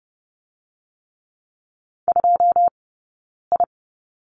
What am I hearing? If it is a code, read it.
Morse code, heard as a keyed tone, about 30 wpm, 700 Hz. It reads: 2 I
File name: course / 30:10wpm / 099 Lesson 17d - Letter u - Two Characters 30wpm